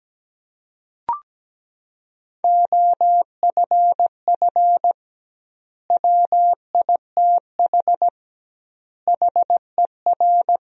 Morse code, heard as a keyed tone, about 17 wpm, 700 Hz. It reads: OFF WITH HER
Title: 2x Word-Spacing - Sentences from Top 300 Words - 17wpm